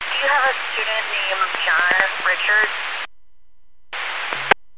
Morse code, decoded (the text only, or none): none